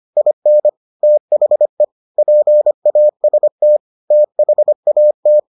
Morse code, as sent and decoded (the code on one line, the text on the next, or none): .. -. / - .... / .--. .- ... - / - .... .- -
IN THE PAST THAT